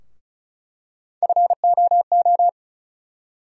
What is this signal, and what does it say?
Morse code, heard as a keyed tone, about 36 wpm, 700 Hz. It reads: FOO
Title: USA Call Sign Postfixes - 3 Letters 35wpm